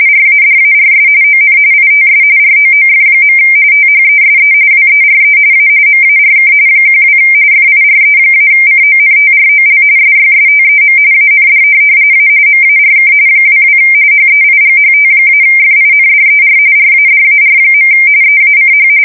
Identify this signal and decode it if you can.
RTTY, about 45.45 baud. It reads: RADARS ON ITS ROOF. THESE WERE A MARK 12 FIRE-CONTROL SYSTEM AND A MARK 22 HEIGHT-FINDER RADAR. EACH
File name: fsk170 / USS-Missouri